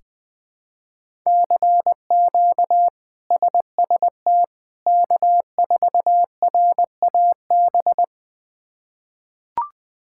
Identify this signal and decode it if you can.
Morse code, heard as a keyed tone, about 20 wpm, 700 Hz. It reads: CQ SST K4RAB
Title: K1USN SST QSO Practice - 20wpm